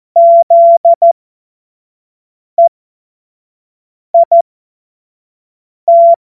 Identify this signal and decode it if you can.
Morse code, heard as a keyed tone, about 14 wpm, spaced as for 5 wpm, 680 Hz. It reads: ZEIT
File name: Morsekurs / Witze / Witz 13 Mai 2025 25 F70